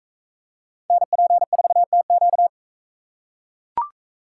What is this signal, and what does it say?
Morse code, heard as a keyed tone, about 40 wpm, 700 Hz. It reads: NP4TQ